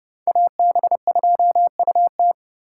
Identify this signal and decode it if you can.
Morse code, heard as a keyed tone, about 30 wpm, 700 Hz. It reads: AB2UT